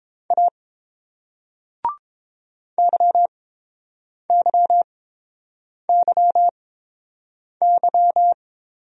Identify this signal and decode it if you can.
Morse code, heard as a keyed tone, about 31 wpm, spaced as for 7 wpm, 690 Hz. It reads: A YYYY